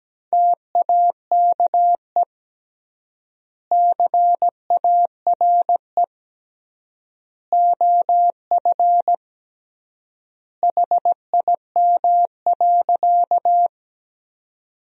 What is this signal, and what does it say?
Morse code, heard as a keyed tone, about 17 wpm, 700 Hz. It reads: TAKE CARE OF HIM.
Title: 3x Word-Spacing - Sentences from Top 300 Words - 17wpm